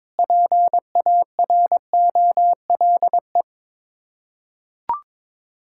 Morse code, heard as a keyed tone, about 22 wpm, 700 Hz. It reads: PAROLE
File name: practice / 22wpm / Sets of 8 Words - 22wpm